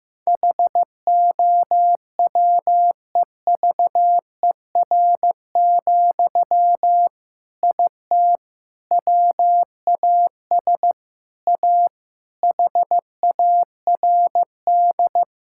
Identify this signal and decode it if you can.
Morse code, heard as a keyed tone, about 16 wpm, 700 Hz. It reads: HOWEVER, IT WAS A HARD
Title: Sentences from Top 1000 Words - Review Part 6 - 15wpm